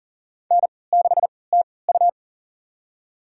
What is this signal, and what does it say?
Morse code, heard as a keyed tone, about 40 wpm, spaced as for 23 wpm, 700 Hz. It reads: N6TU
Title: USA Call Signs - 4 Characters - 3x Character Spacing - 40wpm